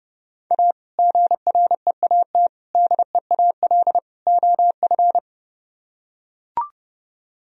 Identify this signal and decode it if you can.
Morse code, heard as a keyed tone, about 30 wpm, 700 Hz. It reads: A GREAT DEAL OF